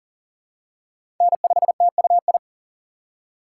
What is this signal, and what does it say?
Morse code, heard as a keyed tone, about 40 wpm, 700 Hz. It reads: N5TUI
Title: USA Call Signs - 5 Characters - 6x Character Spacing - 40wpm